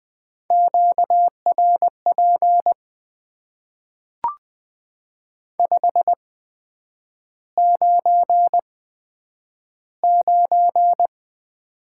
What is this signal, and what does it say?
Morse code, heard as a keyed tone, about 20 wpm, spaced as for 9 wpm, 700 Hz. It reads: QRP 599
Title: Top 90 QSO Elements - 8x Character Spacing - 20wpm